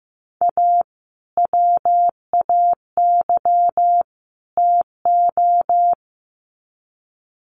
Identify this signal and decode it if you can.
Morse code, heard as a keyed tone, about 15 wpm, 700 Hz. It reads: A WAY TO
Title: Sets of 3 Words 15wpm